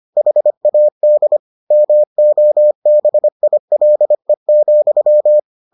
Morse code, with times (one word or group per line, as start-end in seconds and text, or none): 0.2-1.4 HAD
1.7-5.4 MOBILE,